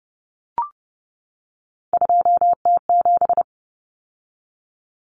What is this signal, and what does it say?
Morse code, heard as a keyed tone, about 30 wpm, 700 Hz. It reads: 2T7
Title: Common Cut Numbers - 3-Digit 30wpm